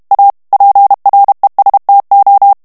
Morse code, heard as a keyed tone, about 32 wpm, 800 Hz. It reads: A PRESTO
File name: ITALIEN texte clair 1 vitesse 30 mots minute